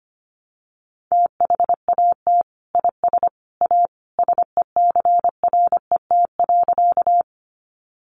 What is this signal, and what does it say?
Morse code, heard as a keyed tone, about 25 wpm, 700 Hz. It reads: THAT IS A SECRET.
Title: Sentences from Top 1200 Words 25wpm